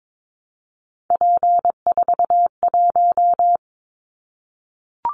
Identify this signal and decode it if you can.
Morse code, heard as a keyed tone, about 22 wpm, 700 Hz. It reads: P41
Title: Callsign Prefixes for Top 100 DXCC - 22wpm